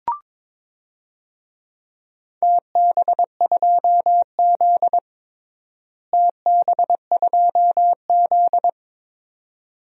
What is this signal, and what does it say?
Morse code, heard as a keyed tone, about 22 wpm, 700 Hz. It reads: TB2Z TB2Z
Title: DXCC-100 Call Signs - 4 Characters - Repeated 2x - 22wpm